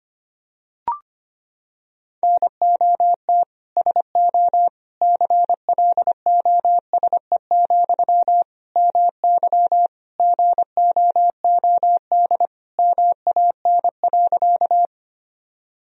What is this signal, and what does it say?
Morse code, heard as a keyed tone, about 25 wpm, 700 Hz. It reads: NOT SO CLOSE, MY GOOD MAN.
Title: Sentences from Top 1000 Words - Review Part 1 - 25wpm